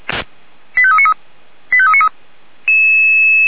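Five-tone selective call (ZVEI), heard as quick runs of five tones, tones of about 70 ms. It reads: 85393 85393
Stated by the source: Five-tone ZVEI selective call received on an SDR, channel or frequency 464.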